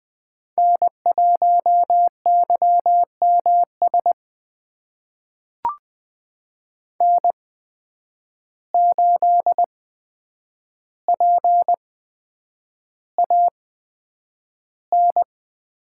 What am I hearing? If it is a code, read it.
Morse code, heard as a keyed tone, about 20 wpm, spaced as for 9 wpm, 700 Hz. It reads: N1YMS N8PAN
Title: USA Call Signs - 5 Characters - 8x Character Spacing - 20wpm